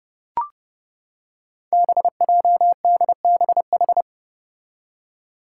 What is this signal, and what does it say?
Morse code, heard as a keyed tone, about 32 wpm, 700 Hz. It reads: BJDBH
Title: Five Challenging Letter-Number Combinations 30wpm